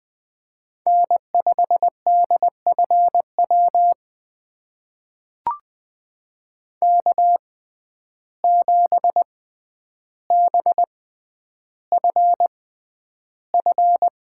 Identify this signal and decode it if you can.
Morse code, heard as a keyed tone, about 20 wpm, spaced as for 10 wpm, 700 Hz. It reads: N5DFW K7BFF
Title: USA Call Signs - 5 Characters - 6x Character Spacing - 20wpm